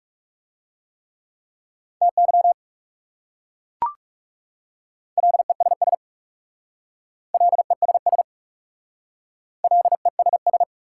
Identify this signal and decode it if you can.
Morse code, heard as a keyed tone, about 39 wpm, 700 Hz. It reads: TY LESS LESS LESS